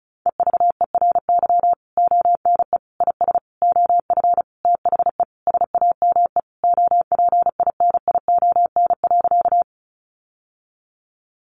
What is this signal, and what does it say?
Morse code, heard as a keyed tone, about 34 wpm, 700 Hz. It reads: EVERY ONE IS OF THE SAME OPINION.
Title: Sentences from Top 1200 Words 35wpm